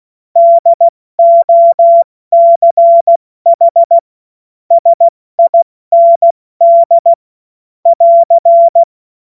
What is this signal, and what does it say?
Morse code, heard as a keyed tone, about 16 wpm, 680 Hz. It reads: DOCH SIND <AR>